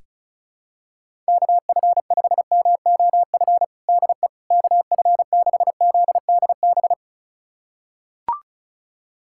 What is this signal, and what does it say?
Morse code, heard as a keyed tone, about 34 wpm, 700 Hz. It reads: KF5MOF DE KF6ZDB
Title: USA Callsign DE Callsign - 6 Characters - 35wpm